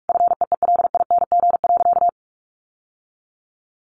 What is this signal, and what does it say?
Morse code, heard as a keyed tone, about 44 wpm, 700 Hz. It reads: FEELING.
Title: Sentences from Top 1400 Words 45wpm